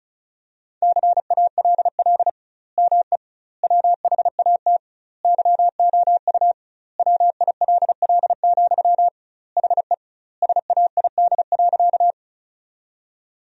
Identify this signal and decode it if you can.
Morse code, heard as a keyed tone, about 36 wpm, 700 Hz. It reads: CALL ME WHAT YOU WILL, HE SAID.
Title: 2x Word-Spacing - Sentences from Top 200 Words - 35wpm